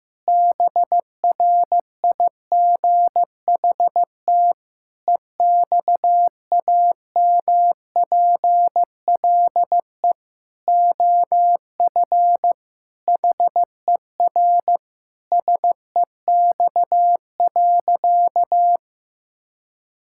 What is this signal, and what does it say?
Morse code, heard as a keyed tone, about 15 wpm, 700 Hz. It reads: BRIGHT EXAMPLE OF HER SEX.
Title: Sentences from Top 1600 Words 15wpm